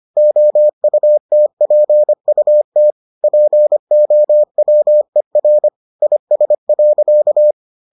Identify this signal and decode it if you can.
Morse code, heard as a keyed tone, about 25 wpm, 600 Hz. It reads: OUTPUT POWER IS.